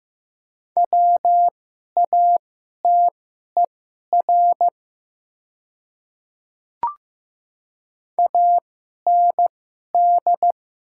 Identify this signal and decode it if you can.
Morse code, heard as a keyed tone, about 15 wpm, spaced as for 11 wpm, 700 Hz. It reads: WATER AND
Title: 2x Character Spacing - Top 200 Words - 15wpm